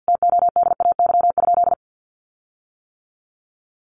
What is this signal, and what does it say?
Morse code, heard as a keyed tone, about 50 wpm, 700 Hz. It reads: TODAY?
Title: Sentences from Top 1400 Words 50wpm